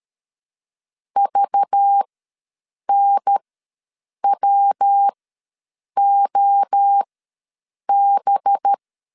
Morse code, heard as a keyed tone, about 13 wpm, spaced as for 7 wpm, 800 Hz. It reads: VNWOB
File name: LEZIONI / 5 LBVC / 5 LBVC ws 12wpm 3159